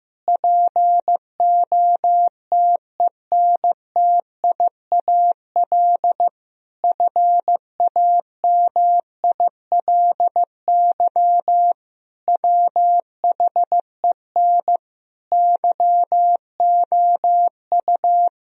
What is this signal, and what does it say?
Morse code, heard as a keyed tone, about 15 wpm, 700 Hz. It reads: POTENTIAL FAMILY WHEN YOU
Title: Sentences from Top 1000 Words - Review Part 4 - 15wpm